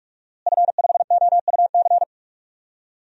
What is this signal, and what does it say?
Morse code, heard as a keyed tone, about 42 wpm, 700 Hz. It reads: F5OUC
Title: DXCC-100 Call Signs - 5 Characters - Repeated 2x - 45wpm